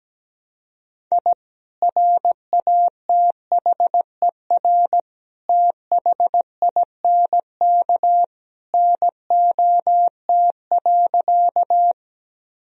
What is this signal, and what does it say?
Morse code, heard as a keyed tone, about 17 wpm, 700 Hz. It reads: I RATHER THINK NOT.
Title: Sentences from Top 1000 Words - Review Part 4 - 17wpm